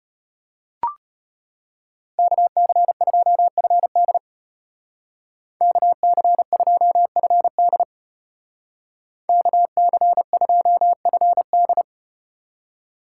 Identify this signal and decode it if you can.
Morse code, heard as a keyed tone, about 34 wpm, 700 Hz. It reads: KC2FD KC2FD KC2FD